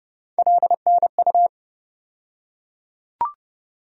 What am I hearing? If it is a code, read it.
Morse code, heard as a keyed tone, about 30 wpm, 700 Hz. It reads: LNU
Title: USA Call Sign Postfixes - 3 Letters 30wpm